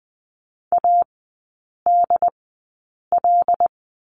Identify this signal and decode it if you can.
Morse code, heard as a keyed tone, about 20 wpm, spaced as for 8 wpm, 700 Hz. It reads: ADL
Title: 074 Lesson 13e - Letter L - Three Characters 20wpm